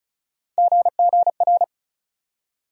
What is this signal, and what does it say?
Morse code, heard as a keyed tone, about 36 wpm, 700 Hz. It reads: GGR
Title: USA Call Sign Postfixes - 3 Letters 35wpm